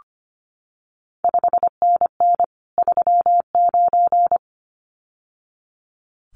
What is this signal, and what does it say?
Morse code, heard as a keyed tone, about 25 wpm, 700 Hz. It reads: 5NN 39
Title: CQ WW DX Contest Exchange Encore - 25wpm